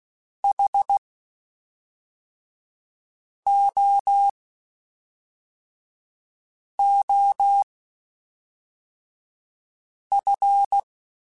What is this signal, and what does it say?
Morse code, heard as a keyed tone, about 16 wpm, spaced as for 3 wpm, 800 Hz. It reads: HOOF
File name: Quotes Spaced / Quotes Spaced-mp3s / HoofInMouth sp @5